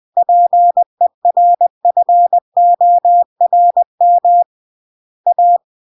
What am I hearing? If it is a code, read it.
Morse code, heard as a keyed tone, about 20 wpm, 700 Hz. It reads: PERFORM A
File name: CW / jodorowsky / 82 commandments 20-20